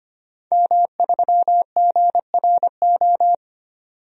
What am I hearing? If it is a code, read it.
Morse code, heard as a keyed tone, about 25 wpm, 700 Hz. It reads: M3GRO
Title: DXCC-100 Call Signs - 5 Characters - 25wpm